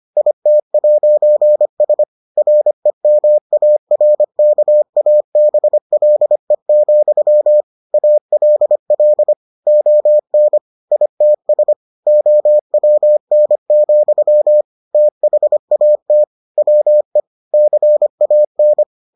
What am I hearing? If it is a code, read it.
Morse code, heard as a keyed tone, about 25 wpm, 600 Hz. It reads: IT'S REMARKABLE, ALL ON ITS OWN, THAT WE CAN